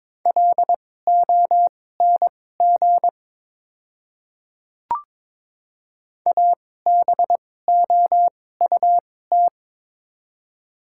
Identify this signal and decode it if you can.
Morse code, heard as a keyed tone, about 22 wpm, spaced as for 16 wpm, 700 Hz. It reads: LONG ABOUT
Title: 2x Character Spacing - Top 100 Words - 22wpm